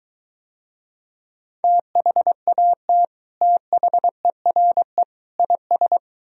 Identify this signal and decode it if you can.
Morse code, heard as a keyed tone, about 23 wpm, 700 Hz. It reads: THAT THERE IS